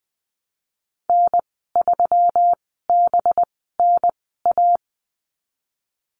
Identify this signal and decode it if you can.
Morse code, heard as a keyed tone, about 20 wpm, spaced as for 15 wpm, 700 Hz. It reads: N3BNA
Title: 213 Lesson 34g - Number 8 - Call Signs 20wpm